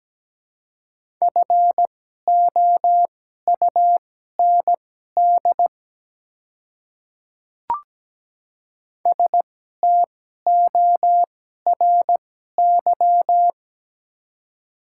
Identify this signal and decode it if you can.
Morse code, heard as a keyed tone, about 17 wpm, spaced as for 12 wpm, 700 Hz. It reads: FOUND STORY
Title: 2x Character Spacing - Top 300 Words - 17wpm